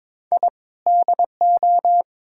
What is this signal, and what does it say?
Morse code, heard as a keyed tone, about 21 wpm, 700 Hz. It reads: I DO
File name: practice / 22wpm / Build A Phrase - 5-Word Phrases - 22wpm